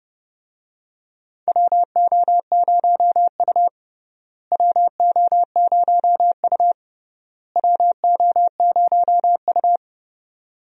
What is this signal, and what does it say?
Morse code, heard as a keyed tone, about 30 wpm, 700 Hz. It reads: WO0U WO0U WO0U
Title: USA Call Signs - 4 Characters - Repeated 3x - 30wpm